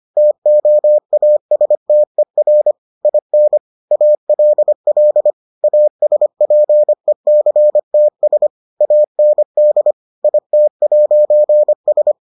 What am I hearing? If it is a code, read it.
Morse code, heard as a keyed tone, about 25 wpm, 600 Hz. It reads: TOASTER IN ALL ASPECTS AND IT'S